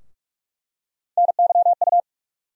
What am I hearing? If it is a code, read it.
Morse code, heard as a keyed tone, about 46 wpm, 700 Hz. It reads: NYU
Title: USA Call Sign Postfixes - 3 Letters 45wpm